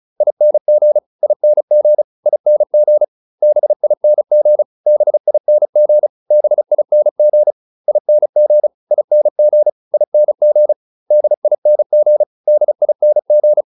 Morse code, heard as a keyed tone, about 34 wpm, 600 Hz. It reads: ING ING ING BING BING BING ING ING ING DING DING